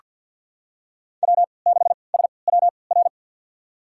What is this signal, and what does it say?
Morse code, heard as a keyed tone, about 48 wpm, spaced as for 28 wpm, 700 Hz. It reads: W6SWR
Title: USA Call Signs - 5 Characters - 3x Character Spacing - 50wpm